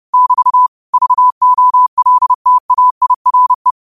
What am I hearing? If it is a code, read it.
Morse code, heard as a keyed tone, about 30 wpm, 1000 Hz. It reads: <BT> VOLTAIRE